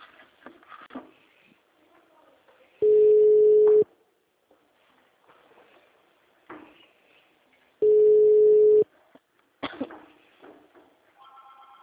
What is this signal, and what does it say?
Ringback tone (Europe, 425 Hz).